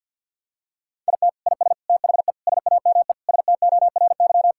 Morse code, heard as a keyed tone, about 45 wpm, 700 Hz. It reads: IT IS THE SAME STORY
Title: Sentences from Top 1600 Words 50wpm